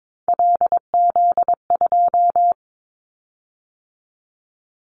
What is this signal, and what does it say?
Morse code, heard as a keyed tone, about 22 wpm, 700 Hz. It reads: LZ2
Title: Callsign Prefixes for Top 100 DXCC - 22wpm